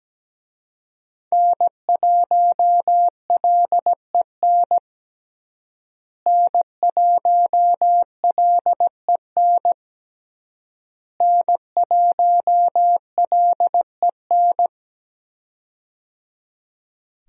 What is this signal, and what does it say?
Morse code, heard as a keyed tone, about 17 wpm, 700 Hz. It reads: N1LEN N1LEN N1LEN
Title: USA Call Signs - 5 Characters - Repeated 3x - 17wpm